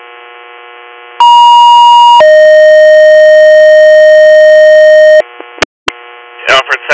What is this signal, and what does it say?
Two-tone page, heard as a one-second tone, then a three-second tone, about 950 then 620 Hz, with speech. It is heard from 1.2 to 5.2 s.